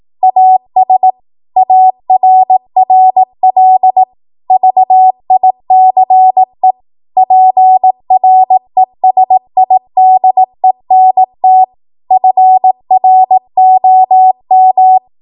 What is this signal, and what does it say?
Morse code, heard as a keyed tone, about 18 wpm, 750 Hz. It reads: AS ARRL VICE PRESIDENT FROM